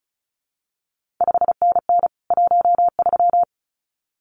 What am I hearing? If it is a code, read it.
Morse code, heard as a keyed tone, about 36 wpm, 700 Hz. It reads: 5NN 13
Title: CQ WW DX Contest Exchange Encore - 35wpm